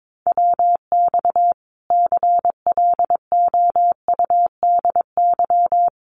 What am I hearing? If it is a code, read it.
Morse code, heard as a keyed tone, about 22 wpm, 700 Hz. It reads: WX CLOUDY
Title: Common QSO Phrases 22wpm